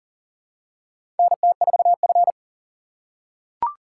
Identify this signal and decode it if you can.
Morse code, heard as a keyed tone, about 40 wpm, 700 Hz. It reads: NT4F
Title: USA Call Signs - 4 Characters - 3x Character Spacing - 40wpm